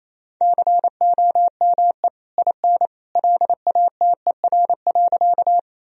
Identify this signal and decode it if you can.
Morse code, heard as a keyed tone, about 28 wpm, 700 Hz. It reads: COME IN LATER.
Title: Sentences from Top 1000 Words - Review Part 4 - 28wpm